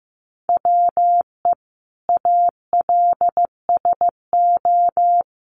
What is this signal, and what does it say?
Morse code, heard as a keyed tone, about 15 wpm, 700 Hz. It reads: WE ALSO
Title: Sentences from Top 1200 Words 15wpm